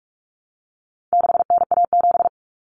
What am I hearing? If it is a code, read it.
Morse code, heard as a keyed tone, about 44 wpm, 700 Hz. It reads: 6NA7
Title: Common Cut Numbers - 4-Digit 45wpm